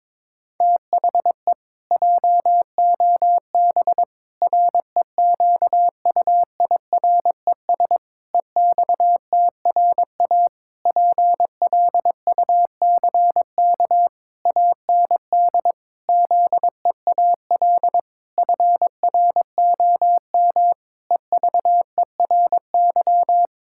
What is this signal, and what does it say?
Morse code, heard as a keyed tone, about 22 wpm, 700 Hz. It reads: THE JOB REQUIRES EXTRA PLUCK AND ZEAL FROM EVERY
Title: Pangram Challenge v2 - 22wpm